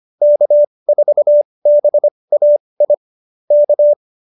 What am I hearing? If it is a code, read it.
Morse code, heard as a keyed tone, about 25 wpm, spaced as for 20 wpm, 590 Hz. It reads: K4BAI K
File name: QSO301 20